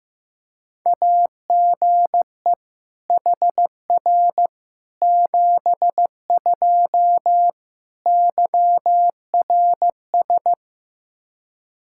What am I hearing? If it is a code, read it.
Morse code, heard as a keyed tone, about 15 wpm, 700 Hz. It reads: AGE HR 72 YRS
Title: K1USN SST QSO Practice 15wpm